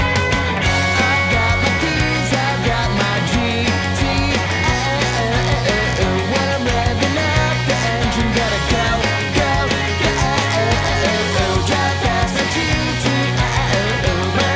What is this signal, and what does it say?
Dial-up modem handshake.